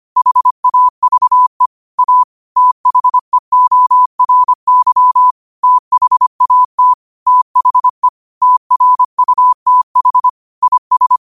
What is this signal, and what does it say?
Morse code, heard as a keyed tone, about 25 wpm, 1000 Hz. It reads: HAVE A THEORY THAT THE TRUTH IS